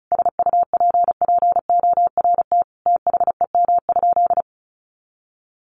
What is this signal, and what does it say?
Morse code, heard as a keyed tone, about 34 wpm, 700 Hz. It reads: SUPPORT THEM?
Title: Sentences from Top 800 Words 35wpm